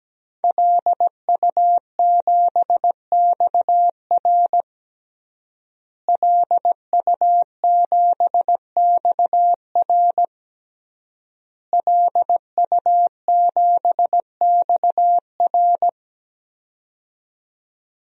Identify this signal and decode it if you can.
Morse code, heard as a keyed tone, about 17 wpm, 700 Hz. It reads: LU7XR LU7XR LU7XR
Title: DXCC-100 Call Signs - 5 Characters - Repeated 3x - 17wpm